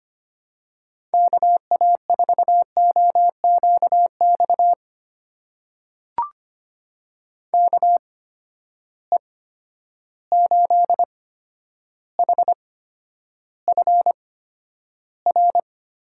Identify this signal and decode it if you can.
Morse code, heard as a keyed tone, about 25 wpm, spaced as for 11 wpm, 700 Hz. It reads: KA4OQX KE8HFR
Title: USA Call Signs - 6 Characters - 8x Character Spacing - 25wpm